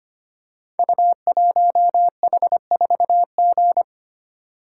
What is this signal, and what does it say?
Morse code, heard as a keyed tone, about 25 wpm, 700 Hz. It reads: U1H4G
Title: Code Groups - 5 Letter:Numbers - 25wpm